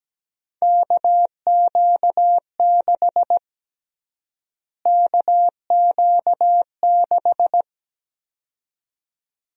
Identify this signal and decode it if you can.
Morse code, heard as a keyed tone, about 17 wpm, 700 Hz. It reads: KQ6 KQ6
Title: USA Call Sign Prefixes - Repeated 2x - 17wpm